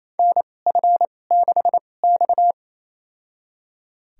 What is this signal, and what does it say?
Morse code, heard as a keyed tone, about 28 wpm, spaced as for 20 wpm, 700 Hz. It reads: NF6X
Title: USA Call Signs - 2x Character Spacing - 28wpm